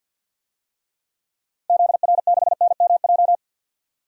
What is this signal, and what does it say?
Morse code, heard as a keyed tone, about 48 wpm, 700 Hz. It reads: ZR6NMJ
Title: DXCC-100 Call Signs - 6 Characters - 50wpm